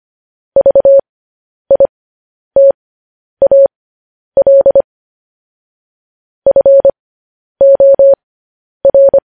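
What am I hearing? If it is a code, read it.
Morse code, heard as a keyed tone, about 25 wpm, spaced as for 10 wpm, 550 Hz. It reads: VITAL FOR
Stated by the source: Morse code news headlines broadcast, opening with vvv.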